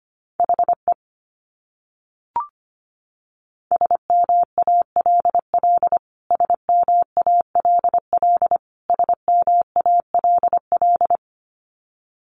Morse code, heard as a keyed tone, about 25 wpm, 700 Hz. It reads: HE SMALL SMALL SMALL
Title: Top 300 Words Repeated 25wpm